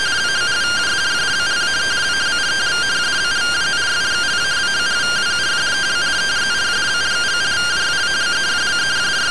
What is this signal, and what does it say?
Two-tone teleprinter signal, about 170 Hz shift.
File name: findings / CT AXIAL -142.gen-20 rtty rx